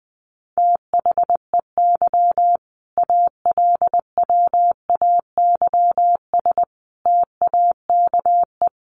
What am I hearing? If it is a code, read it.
Morse code, heard as a keyed tone, about 20 wpm, 700 Hz. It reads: THEY ALWAYS TAKE
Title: Sentences from Top 900 Words 20wpm